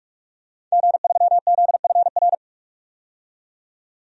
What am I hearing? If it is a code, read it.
Morse code, heard as a keyed tone, about 46 wpm, 700 Hz. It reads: G3ZFR
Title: DXCC-100 Call Signs - 5 Characters - 45wpm